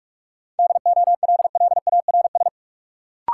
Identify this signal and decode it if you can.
Morse code, heard as a keyed tone, about 46 wpm, 700 Hz. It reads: DOLLARS
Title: Sets of 5 Words - New Word Drill - 45wpm